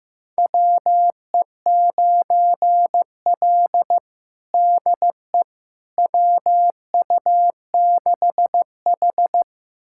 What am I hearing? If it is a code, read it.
Morse code, heard as a keyed tone, about 15 wpm, 700 Hz. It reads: WE9L DE WU6H
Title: USA Callsign DE Callsign - 4 Characters - 15wpm